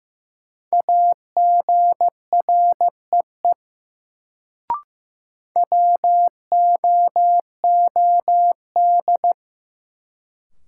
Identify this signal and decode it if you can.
Morse code, heard as a keyed tone, about 15 wpm, 700 Hz. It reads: AGREE WOOD
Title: Sentences from Top 1100 Words - New Word Drill Alt - 15wpm